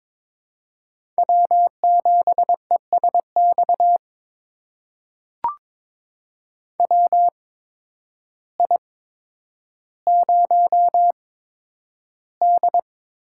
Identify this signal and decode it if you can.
Morse code, heard as a keyed tone, about 22 wpm, spaced as for 10 wpm, 700 Hz. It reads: W7ESX WI0D
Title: USA Call Signs - 8x Character Spacing - 22wpm